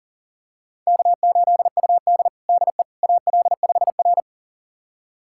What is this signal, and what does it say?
Morse code, heard as a keyed tone, about 40 wpm, 700 Hz. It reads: K8UD DE AL5R
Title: USA Callsign DE Callsign - 4 Characters - 40wpm